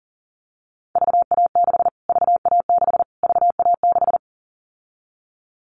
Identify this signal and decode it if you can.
Morse code, heard as a keyed tone, about 40 wpm, 700 Hz. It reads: VA6 VA6 VA6
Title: Callsign Prefixes for Top 25 DXCC - Repeated 3x - 40wpm